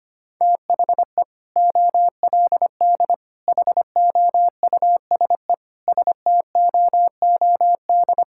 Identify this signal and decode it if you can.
Morse code, heard as a keyed tone, about 25 wpm, 690 Hz. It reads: THE OLD HOUSE STOOD
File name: practice / 25wpm / Sentences with 2-Word Phrases - 25wpm